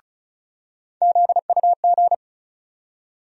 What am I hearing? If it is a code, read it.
Morse code, heard as a keyed tone, about 36 wpm, 700 Hz. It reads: ZUG
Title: USA Call Sign Postfixes - 3 Letters 35wpm